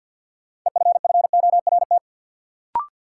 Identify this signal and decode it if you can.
Morse code, heard as a keyed tone, about 51 wpm, 700 Hz. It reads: EFFORT
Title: Sentences from Top 1200 Words - New Word Drill Alt - 50wpm